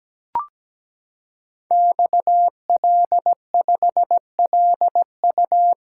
Morse code, heard as a keyed tone, about 17 wpm, 700 Hz. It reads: XL5LU
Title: Code Groups - 5 Letter:Numbers - 17wpm